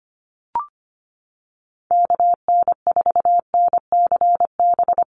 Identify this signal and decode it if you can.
Morse code, heard as a keyed tone, about 25 wpm, 700 Hz. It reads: KN4NCB
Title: USA Call Signs Repeated 3x 25wpm